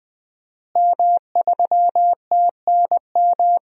Morse code, heard as a keyed tone, about 20 wpm, 700 Hz. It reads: M3TNM
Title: DXCC-100 Call Signs - 5 Characters - Repeated 2x - 20wpm